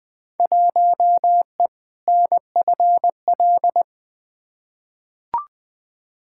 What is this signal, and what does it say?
Morse code, heard as a keyed tone, about 20 wpm, 700 Hz. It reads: 1E NFL
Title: Field Day Exchanges - 20wpm